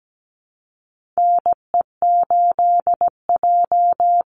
Morse code, heard as a keyed tone, about 17 wpm, 700 Hz. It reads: NE8J
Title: Call Signs Repeated 3x - 17wpm